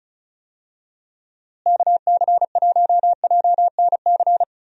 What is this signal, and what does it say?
Morse code, heard as a keyed tone, about 36 wpm, 700 Hz. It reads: KC1JNC
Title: USA Call Signs - 6 Characters - Repeated 2x - 35wpm